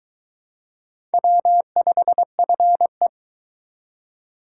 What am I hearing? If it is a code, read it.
Morse code, heard as a keyed tone, about 22 wpm, 700 Hz. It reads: W5FE